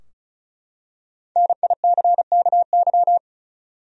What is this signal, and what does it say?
Morse code, heard as a keyed tone, about 36 wpm, 700 Hz. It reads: NICKY
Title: USA Common Names - 35wpm